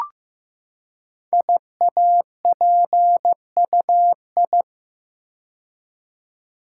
Morse code, heard as a keyed tone, about 15 wpm, 700 Hz. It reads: IAPUI
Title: Code Groups - 5 Letters - 15wpm